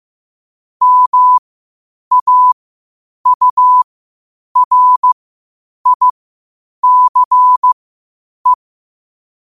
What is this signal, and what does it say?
Morse code, heard as a keyed tone, about 15 wpm, spaced as for 8 wpm, 1000 Hz. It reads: MAURICE
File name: qotdfiles / 8wpm